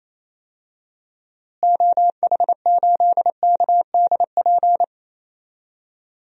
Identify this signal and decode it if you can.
Morse code, heard as a keyed tone, about 28 wpm, 700 Hz. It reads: OH8KDP
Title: DXCC-100 Call Signs - 6 Characters - Repeated 2x - 28wpm